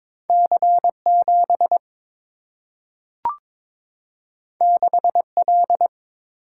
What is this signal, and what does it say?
Morse code, heard as a keyed tone, about 22 wpm, 700 Hz. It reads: C 7 6 L
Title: Single-Letter Single-Number 22wpm